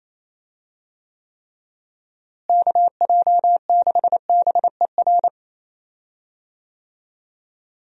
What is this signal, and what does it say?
Morse code, heard as a keyed tone, about 28 wpm, 700 Hz. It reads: KJ6BER
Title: USA Call Signs - 6 Characters - Repeated 3x - 28wpm